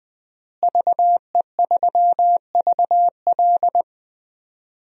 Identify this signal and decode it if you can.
Morse code, heard as a keyed tone, about 20 wpm, 700 Hz. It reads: VE3VL
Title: DXCC-100 Call Signs - 5 Characters - Repeated 2x - 20wpm